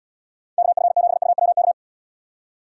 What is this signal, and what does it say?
Morse code, heard as a keyed tone, about 78 wpm, 700 Hz. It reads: KF7UWK